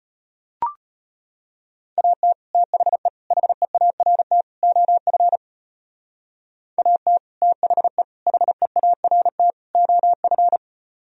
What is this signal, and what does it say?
Morse code, heard as a keyed tone, about 31 wpm, 700 Hz. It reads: AT THE HEART OF AT THE HEART OF
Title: Speed-Racing - Sets of 4 Words 25wpm